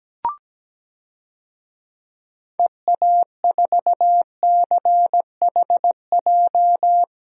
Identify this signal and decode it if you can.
Morse code, heard as a keyed tone, about 17 wpm, 700 Hz. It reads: EA4CHJ
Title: DXCC-100 Call Signs - 6 Characters - 17wpm